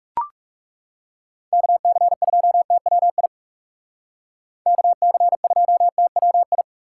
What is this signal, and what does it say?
Morse code, heard as a keyed tone, about 38 wpm, 700 Hz. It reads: KC2TWI KC2TWI
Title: Speed Racing - USA Call Signs 30wpm